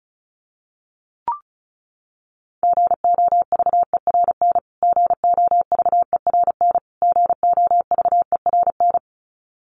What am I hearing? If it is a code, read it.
Morse code, heard as a keyed tone, about 34 wpm, 700 Hz. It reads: GOVERN GOVERN GOVERN
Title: Top 500 Words Repeated 35wpm